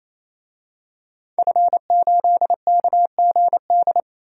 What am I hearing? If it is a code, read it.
Morse code, heard as a keyed tone, about 28 wpm, 700 Hz. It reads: F8KGD
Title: DXCC-100 Call Signs - 5 Characters - 28wpm